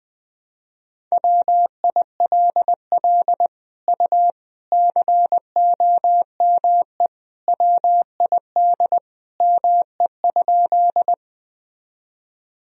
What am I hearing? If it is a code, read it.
Morse code, heard as a keyed tone, about 20 wpm, 700 Hz. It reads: WILL U COME WID ME?
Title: Sentences from Top 200 Words and Abbreviations 20wpm